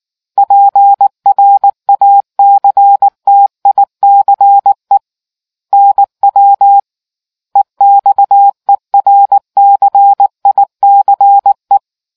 Morse code, heard as a keyed tone, about 19 wpm, 800 Hz. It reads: PRACTICE NW EXERCICE